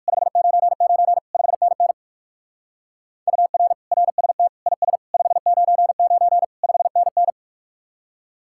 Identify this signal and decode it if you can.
Morse code, heard as a keyed tone, about 64 wpm, spaced as for 56 wpm, 700 Hz. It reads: 599 5NN UR RST IS 599 5NN
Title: Speed-Racing Common QSO Phrases Encore - 45wpm